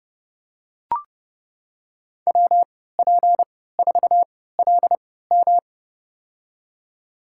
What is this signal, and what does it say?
Morse code, heard as a keyed tone, about 30 wpm, spaced as for 17 wpm, 700 Hz. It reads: WP4LM
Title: USA Call Signs - 3x Character Spacing - 30wpm